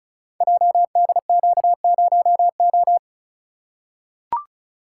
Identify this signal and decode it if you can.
Morse code, heard as a keyed tone, about 34 wpm, 700 Hz. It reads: JDQ0O
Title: Code Groups - 5 Letter:Numbers - 35wpm